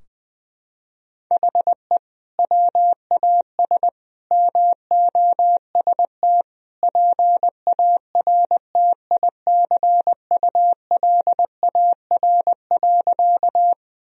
Morse code, heard as a keyed tone, about 21 wpm, 700 Hz. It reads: HE WAS MOST PARTICULAR.
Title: Sentences from Top 1000 Words - Review Part 4 - 20wpm